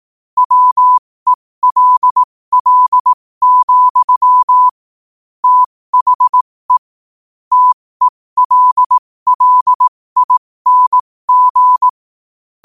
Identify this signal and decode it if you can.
Morse code, heard as a keyed tone, about 18 wpm, spaced as for 15 wpm, 1000 Hz. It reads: WELL, THE TELLING
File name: qotdfiles / 15wpm